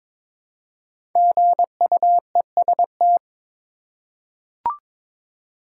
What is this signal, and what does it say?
Morse code, heard as a keyed tone, about 22 wpm, 700 Hz. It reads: GUEST
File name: 5 Letter Words - First 3 Letters Given - 22wpm